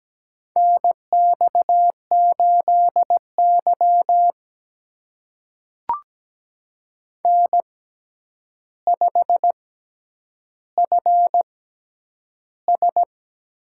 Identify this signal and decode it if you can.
Morse code, heard as a keyed tone, about 17 wpm, spaced as for 9 wpm, 700 Hz. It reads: NX8Y N5FS
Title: USA Call Signs - 4 Characters - 6x Character Spacing - 17wpm